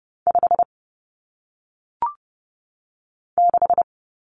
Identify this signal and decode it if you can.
Morse code, heard as a keyed tone, about 30 wpm, 700 Hz. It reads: 5 6
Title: Numbers - 1-Digit 30wpm